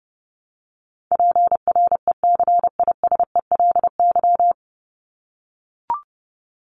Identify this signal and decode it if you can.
Morse code, heard as a keyed tone, about 30 wpm, 700 Hz. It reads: PRECISELY
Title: LY Suffix - Top 100 Words with Base Words 30wpm